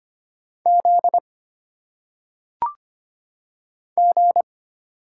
Morse code, heard as a keyed tone, about 25 wpm, 700 Hz. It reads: Z G
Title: Single Letters - Phonetic Pronunciation 25wpm